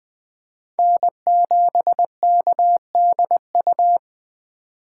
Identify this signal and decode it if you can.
Morse code, heard as a keyed tone, about 20 wpm, 700 Hz. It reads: N7KDU
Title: USA Call Signs - 5 Characters - 8x Character Spacing - 20wpm